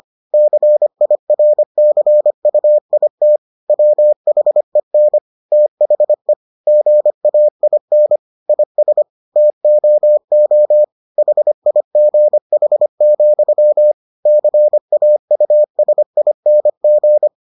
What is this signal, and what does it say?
Morse code, heard as a keyed tone, about 25 wpm, 600 Hz. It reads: CIRCUIT WHEN THE GAIN IS TOO HIGH, CAUSING